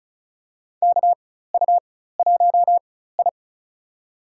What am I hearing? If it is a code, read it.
Morse code, heard as a keyed tone, about 36 wpm, spaced as for 16 wpm, 700 Hz. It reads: KU1I
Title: USA Call Signs - 4 Characters - 4x Character Spacing - 35wpm